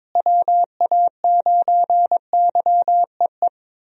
Morse code, heard as a keyed tone, about 22 wpm, 700 Hz. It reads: WA9YEE
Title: USA Call Signs - 6 Characters - Repeated 3x - 22wpm